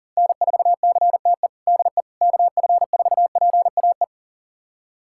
Morse code, heard as a keyed tone, about 40 wpm, 700 Hz. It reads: N4CTE DE KF4PAE
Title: USA Callsign DE Callsign 40wpm